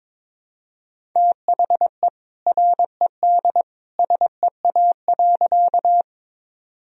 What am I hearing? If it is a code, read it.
Morse code, heard as a keyed tone, about 22 wpm, 700 Hz. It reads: THE RED SEA.
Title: Sentences from Top 1000 Words - Review Part 2 - 22wpm